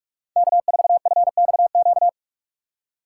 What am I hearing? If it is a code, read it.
Morse code, heard as a keyed tone, about 46 wpm, 700 Hz. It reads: K4FXQ